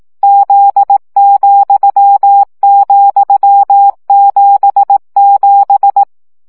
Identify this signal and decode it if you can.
Morse code, heard as a keyed tone, about 18 wpm, 800 Hz. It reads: Z,,77
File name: CW gyakorlas z 7 vesszo 15 baud